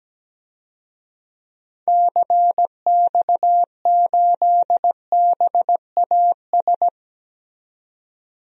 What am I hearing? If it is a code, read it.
Morse code, heard as a keyed tone, about 17 wpm, 700 Hz. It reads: CX8BAS
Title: DXCC-100 Call Signs - 6 Characters - 17wpm